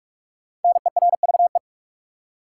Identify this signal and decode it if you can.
Morse code, heard as a keyed tone, about 46 wpm, 700 Hz. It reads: NERVE
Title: Five Letter Words - 45wpm